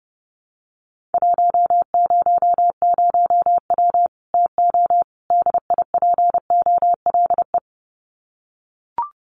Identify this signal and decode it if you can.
Morse code, heard as a keyed tone, about 30 wpm, 700 Hz. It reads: 100W TO DIPOLE
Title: Common QSO Phrases 30wpm